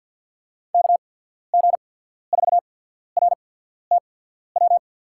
Morse code, heard as a keyed tone, about 51 wpm, spaced as for 14 wpm, 700 Hz. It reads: KG4RTW
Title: USA Call Signs - 6 Characters - 8x Character Spacing - 50wpm